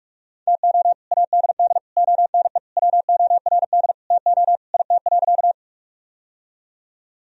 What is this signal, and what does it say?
Morse code, heard as a keyed tone, about 46 wpm, 700 Hz. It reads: TO ADD ONE WORD TO IT.
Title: Sentences from Top 1000 Words - Review Part 1 - 45wpm